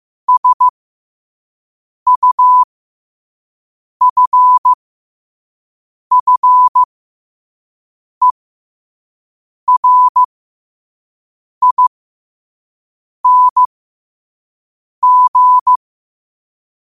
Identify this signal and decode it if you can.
Morse code, heard as a keyed tone, about 15 wpm, spaced as for 5 wpm, 1000 Hz. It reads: SUFFERING
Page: New quotes every day in morse code at 5 Words per minute.